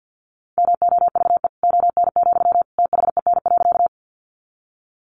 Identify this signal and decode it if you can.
Morse code, heard as a keyed tone, about 50 wpm, 700 Hz. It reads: MOVE ON, THEN.
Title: Sentences from Top 200 Words 50wpm